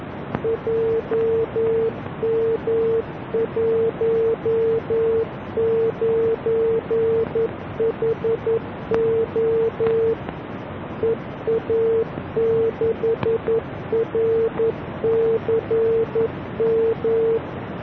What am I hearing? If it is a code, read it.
Morse code, heard as a keyed tone, about 11 wpm, 450 Hz, with noise. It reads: JM19HO EA6RCM